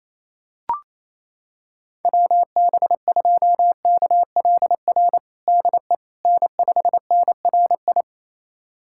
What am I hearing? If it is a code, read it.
Morse code, heard as a keyed tone, about 28 wpm, 700 Hz. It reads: WB2KLR DE N5NRI
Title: USA Callsign DE Callsign 28wpm